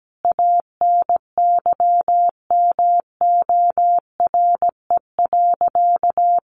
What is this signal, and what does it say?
Morse code, heard as a keyed tone, about 17 wpm, 700 Hz. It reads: ANYMORE.
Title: Additional Easy Phrases 17wpm